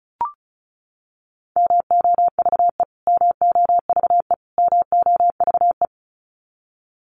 Morse code, heard as a keyed tone, about 34 wpm, 700 Hz. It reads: MOVE MOVE MOVE
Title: Top 500 Words Repeated 35wpm